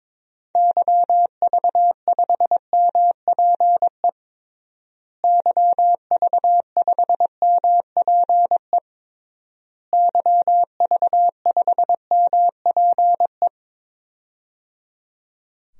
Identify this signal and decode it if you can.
Morse code, heard as a keyed tone, about 22 wpm, 700 Hz. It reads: YV5MPE YV5MPE YV5MPE
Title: DXCC-100 Call Signs - 6 Characters - Repeated 3x - 22wpm